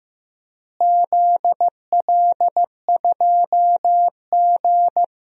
Morse code, heard as a keyed tone, about 15 wpm, 700 Hz. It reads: ZL2G
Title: DXCC-100 Call Signs - 4 Characters - Repeated 3x - 15wpm